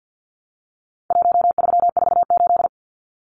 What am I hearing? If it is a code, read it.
Morse code, heard as a keyed tone, about 50 wpm, 700 Hz. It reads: 1348
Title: Numbers - Contest Style - 4-Digit 50wpm